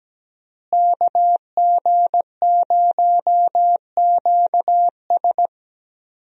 Morse code, heard as a keyed tone, about 17 wpm, 700 Hz. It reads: KG0QS
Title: USA Call Signs - 5 Characters - Repeated 2x - 17wpm